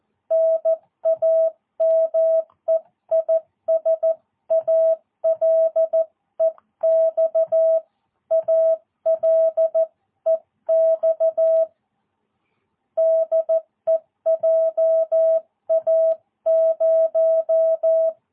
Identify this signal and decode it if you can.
Morse code, heard as a keyed tone, about 14 wpm, 650 Hz. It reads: NAME IS ALEX ALEX DE JA0
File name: jh8cbh / 2022cw